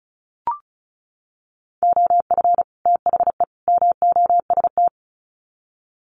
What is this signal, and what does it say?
Morse code, heard as a keyed tone, about 34 wpm, 700 Hz. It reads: OF THE MOST
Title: Sets of 3 Words 35wpm